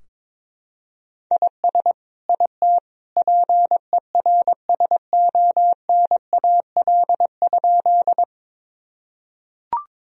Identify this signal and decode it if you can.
Morse code, heard as a keyed tone, about 22 wpm, 700 Hz. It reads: IS IT PERSONAL?
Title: Sentences from Top 1000 Words - Review Part 4 - 22wpm